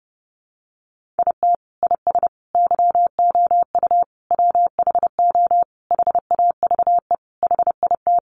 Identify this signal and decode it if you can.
Morse code, heard as a keyed tone, about 30 wpm, 700 Hz. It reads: IT IS YOU WHO HAVE HIT
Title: Sentences from Top 1200 Words 30wpm